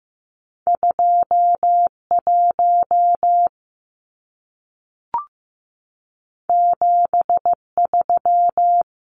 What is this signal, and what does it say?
Morse code, heard as a keyed tone, about 15 wpm, 700 Hz. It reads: 2 1 7 3
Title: Numbers - 2-Digit 15wpm